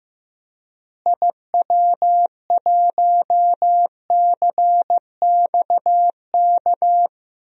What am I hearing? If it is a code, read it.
Morse code, heard as a keyed tone, about 16 wpm, 700 Hz. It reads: IW1CXK
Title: DXCC-100 Call Signs - 6 Characters - Repeated 2x - 15wpm